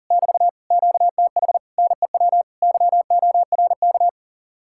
Morse code, heard as a keyed tone, about 50 wpm, 700 Hz. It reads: <BT> QTH NEW YORK